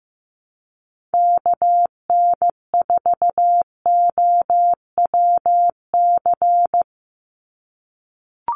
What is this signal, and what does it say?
Morse code, heard as a keyed tone, about 15 wpm, 700 Hz. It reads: KN4OWC
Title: USA Call Signs 15wpm